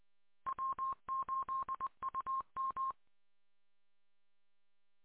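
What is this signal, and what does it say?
Morse code, heard as a keyed tone, about 20 wpm, 1060 Hz. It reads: W8UM
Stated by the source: Morse recorded on 145.230.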